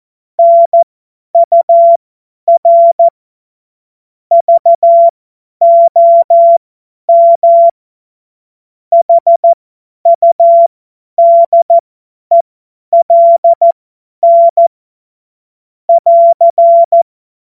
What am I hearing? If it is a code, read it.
Morse code, heard as a keyed tone, about 14 wpm, spaced as for 10 wpm, 680 Hz. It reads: NUR VOM HUDELN <AR>
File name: Morsekurs / Witze / Witz 16. Dezember 2025 50 F70